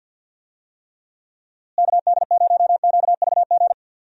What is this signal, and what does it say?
Morse code, heard as a keyed tone, about 46 wpm, 700 Hz. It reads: KD0QVG
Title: USA Call Signs - 6 Characters - Repeated 3x - 50wpm